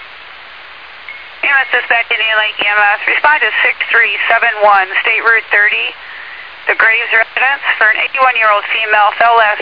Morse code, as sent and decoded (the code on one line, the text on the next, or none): none